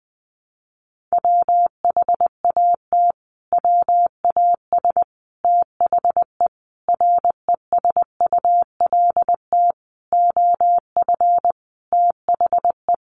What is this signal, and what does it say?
Morse code, heard as a keyed tone, about 20 wpm, 700 Hz. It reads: WHAT WAS THE RESULT OF THE
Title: Sentences from Top 1500 Words 20wpm